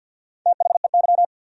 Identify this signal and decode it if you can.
Morse code, heard as a keyed tone, about 53 wpm, 700 Hz. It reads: THEY
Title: 3x Word-Spacing - Sentences from Top 400 Words - 50wpm